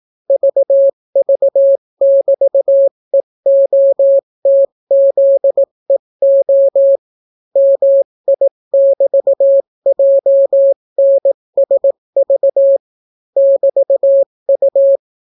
Morse code, heard as a keyed tone, about 18 wpm, 550 Hz. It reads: VV<BT>EOTZEO MI<BT>JNSV <BT>U